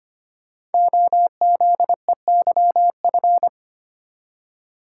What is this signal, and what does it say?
Morse code, heard as a keyed tone, about 25 wpm, 700 Hz. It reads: OZEYF
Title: Code Groups - 5 Letter:Numbers - 25wpm